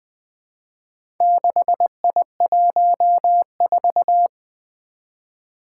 Morse code, heard as a keyed tone, about 20 wpm, 700 Hz. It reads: 6I14
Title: Four Challenging Letter-Number Combinations 20wpm